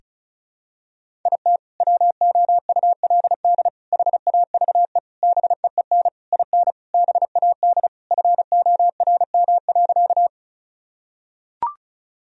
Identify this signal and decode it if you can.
Morse code, heard as a keyed tone, about 34 wpm, 700 Hz. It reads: IT WOULD HAVE BEEN IN BAD FORM.
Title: Sentences from Top 1000 Words - Review Part 1 - 35wpm